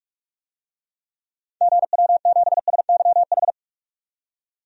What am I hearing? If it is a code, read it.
Morse code, heard as a keyed tone, about 42 wpm, 700 Hz. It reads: GW7SYH
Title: DXCC-100 Call Signs - 6 Characters - 45wpm